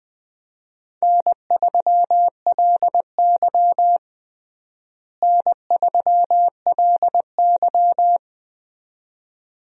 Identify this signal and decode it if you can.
Morse code, heard as a keyed tone, about 20 wpm, 700 Hz. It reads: N3LY N3LY
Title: USA Call Signs - 4 Characters - Repeated 2x - 20wpm